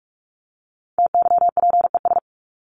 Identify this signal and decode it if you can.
Morse code, heard as a keyed tone, about 44 wpm, 700 Hz. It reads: TYPES
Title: ES:S Suffix - Top 100 Words with Base Words 45wpm